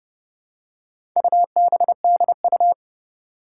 Morse code, heard as a keyed tone, about 29 wpm, 700 Hz. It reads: UBDU
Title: Four Challenging Letter-Number Combinations 30wpm